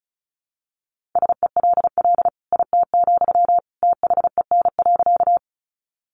Morse code, heard as a keyed tone, about 34 wpm, 700 Hz. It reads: SELL IT, THEN.
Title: Sentences from Top 1100 Words 35wpm